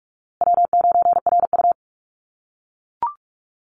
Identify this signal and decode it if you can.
Morse code, heard as a keyed tone, about 44 wpm, 700 Hz. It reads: W9RU
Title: USA Call Signs Repeated 2x 45wpm